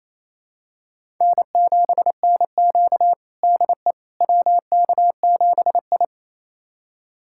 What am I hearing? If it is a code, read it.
Morse code, heard as a keyed tone, about 28 wpm, 700 Hz. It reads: N7NQ DE WK7I
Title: USA Callsign DE Callsign - 4 Characters - 28wpm